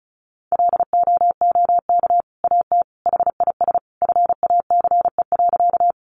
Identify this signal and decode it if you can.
Morse code, heard as a keyed tone, about 34 wpm, 700 Hz. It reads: LOOK AT HIS FACE.
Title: Sentences from Top 300 Words 35wpm